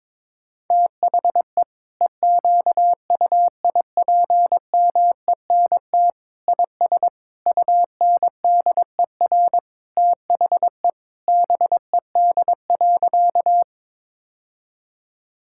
Morse code, heard as a keyed tone, about 22 wpm, 700 Hz. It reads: THE EQUIPMENT IS UNDER THE BED.
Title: Sentences from Top 1000 Words - Review Part 5 - 22wpm